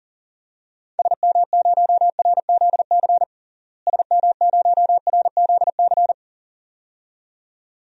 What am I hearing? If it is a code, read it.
Morse code, heard as a keyed tone, about 40 wpm, 700 Hz. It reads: SM0RZC SM0RZC
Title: DXCC-100 Call Signs - 6 Characters - Repeated 2x - 40wpm